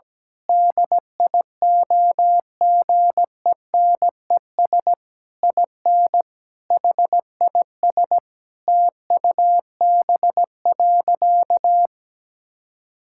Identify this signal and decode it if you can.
Morse code, heard as a keyed tone, about 17 wpm, 700 Hz. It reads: DIOGENES IN HIS TUB.
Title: Pangram Challenge v2 - 17wpm